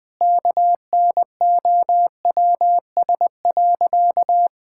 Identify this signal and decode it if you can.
Morse code, heard as a keyed tone, about 21 wpm, 700 Hz. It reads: KNOWS.
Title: Sentences with 3-Word Phrases - 20wpm